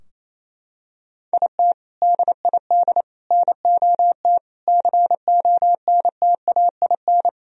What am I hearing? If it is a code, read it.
Morse code, heard as a keyed tone, about 28 wpm, 700 Hz. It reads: IT DID NOT CONTAIN
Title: Sentences from Top 1000 Words - Review Part 2 - 28wpm